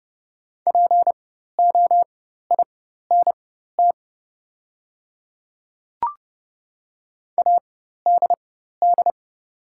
Morse code, heard as a keyed tone, about 30 wpm, spaced as for 14 wpm, 700 Hz. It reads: POINT ADD
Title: 4x Character Spacing - Top 300 Words - 30wpm